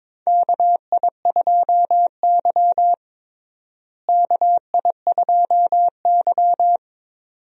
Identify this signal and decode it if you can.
Morse code, heard as a keyed tone, about 22 wpm, 700 Hz. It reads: KI2Y KI2Y
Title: USA Call Signs - 4 Characters - Repeated 2x - 22wpm